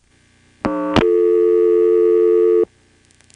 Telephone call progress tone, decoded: dial tone